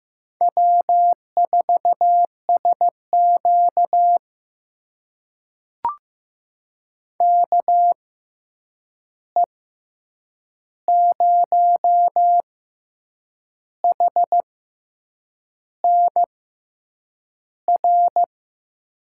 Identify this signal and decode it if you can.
Morse code, heard as a keyed tone, about 15 wpm, spaced as for 7 wpm, 700 Hz. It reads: W4SQ KE0HNR
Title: USA Call Signs - 6x Character Spacing - 15wpm